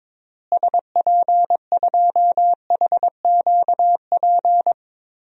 Morse code, heard as a keyed tone, about 22 wpm, 700 Hz. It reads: SP2HQP
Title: DXCC-100 Call Signs - 6 Characters - Repeated 3x - 22wpm